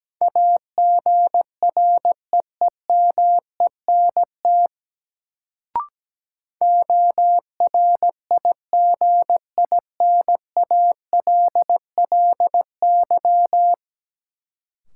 Morse code, heard as a keyed tone, about 17 wpm, 700 Hz. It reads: AGREEMENT ORIGINALLY
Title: Sentences from Top 1500 Words - New Word Drill Alt - 17wpm